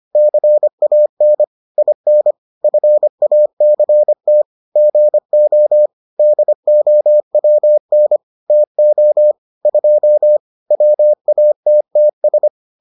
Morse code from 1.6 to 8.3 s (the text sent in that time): IN FACT GO DOWN